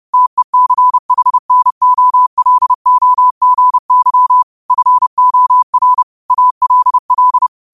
Morse code, heard as a keyed tone, about 30 wpm, 1000 Hz. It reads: TECHNOLOGY FOR ALL
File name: qotdfiles / 30wpm